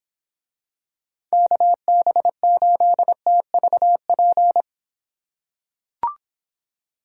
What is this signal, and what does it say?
Morse code, heard as a keyed tone, about 26 wpm, 700 Hz. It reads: KB8TVP